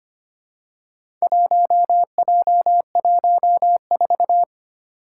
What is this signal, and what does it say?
Morse code, heard as a keyed tone, about 25 wpm, 700 Hz. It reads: 1J14
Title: Four Challenging Letter-Number Combinations 25wpm